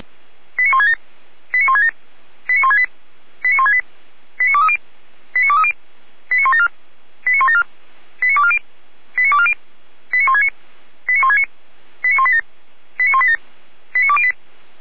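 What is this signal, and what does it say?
Five-tone selective call (ZVEI), heard as quick runs of five tones, tones of about 80 ms. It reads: 89167 89167 89168 89168 89230 89230 89174 89174 89240 89240 89169 89169 89187 89187 89208
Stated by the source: A ZVEI five-tone selective call received by SDR.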